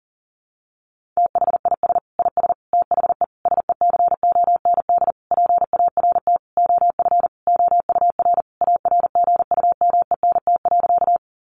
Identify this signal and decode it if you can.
Morse code, heard as a keyed tone, about 37 wpm, 700 Hz. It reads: THIS IS THE SECOND PART OF OUR ARGUMENT.
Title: Sentences from Top 1400 Words 40wpm